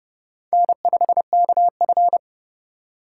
Morse code, heard as a keyed tone, about 29 wpm, 700 Hz. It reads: N5KF
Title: USA Call Signs - 6x Character Spacing - 30wpm